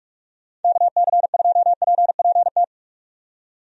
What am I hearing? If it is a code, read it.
Morse code, heard as a keyed tone, about 44 wpm, 700 Hz. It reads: KC2PPT
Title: USA Call Signs - 6 Characters - 45wpm